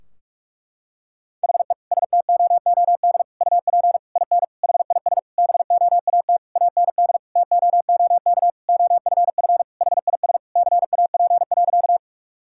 Morse code, heard as a keyed tone, about 46 wpm, 700 Hz. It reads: HE STOOD UP IN HIS BOAT AND TOOK OFF HIS CAP.
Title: Sentences from Top 1600 Words 45wpm